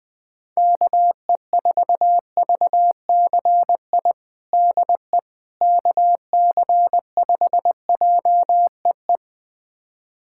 Morse code, heard as a keyed tone, about 21 wpm, 700 Hz. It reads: KE4VCI DE KC5JEE
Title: USA Callsign DE Callsign - 6 Characters - 20wpm